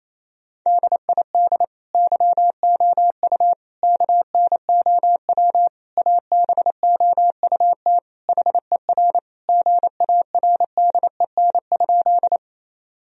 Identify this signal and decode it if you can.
Morse code, heard as a keyed tone, about 28 wpm, 700 Hz. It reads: DID YOU KNOW ABOUT HER GARDEN?
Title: Sentences from Top 1000 Words - Review Part 4 - 28wpm